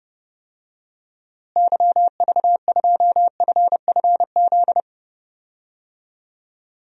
Morse code, heard as a keyed tone, about 30 wpm, 700 Hz. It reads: YV2FFZ
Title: DXCC-100 Call Signs - 6 Characters - Repeated 3x - 30wpm